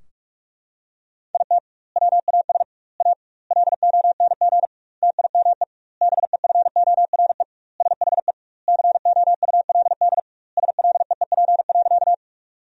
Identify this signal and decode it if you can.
Morse code, heard as a keyed tone, about 41 wpm, 700 Hz. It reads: IT WAS A LONG TIME BEFORE SHE COULD SLEEP.
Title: 2x Word-Spacing - Sentences from Top 500 Words - 45wpm